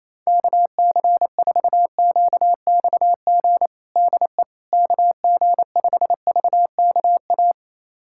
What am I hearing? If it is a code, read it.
Morse code, heard as a keyed tone, about 28 wpm, 700 Hz. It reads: KC4QXG DE KG5VKA
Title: USA Callsign DE Callsign - 6 Characters - 28wpm